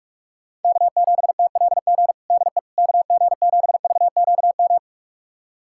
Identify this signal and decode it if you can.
Morse code, heard as a keyed tone, about 46 wpm, 700 Hz. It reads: K7TLG DE KG7VQM
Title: USA Callsign DE Callsign 45wpm